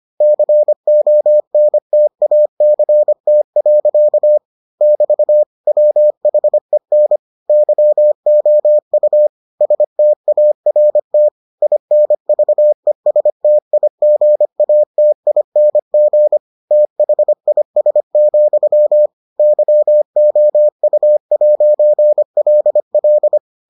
Morse code, heard as a keyed tone, about 25 wpm, 600 Hz. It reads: CONTACT. <BT> WHEN YOU START INVESTIGATING THIS, YOU'LL